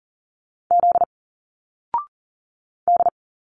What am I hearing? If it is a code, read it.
Morse code, heard as a keyed tone, about 40 wpm, 700 Hz. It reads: Z D
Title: Single Letters 40wpm